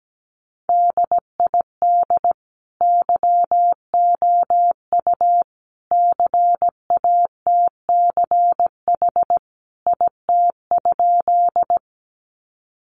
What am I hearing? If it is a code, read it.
Morse code, heard as a keyed tone, about 17 wpm, 700 Hz. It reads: DID YOU CATCH IT?
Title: Sentences from Top 1500 Words 17wpm